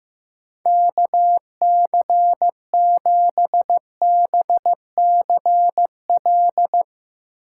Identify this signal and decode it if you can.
Morse code, heard as a keyed tone, about 15 wpm, 700 Hz. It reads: KC7BCL
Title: USA Call Signs - 6 Characters - 4x Character Spacing - 15wpm